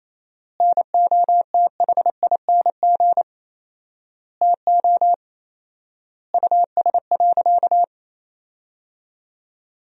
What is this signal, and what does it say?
Morse code, heard as a keyed tone, about 28 wpm, 700 Hz. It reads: NOTHING TO US.
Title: 4x Word-Spacing - Sentences from Top 500 Words - 28wpm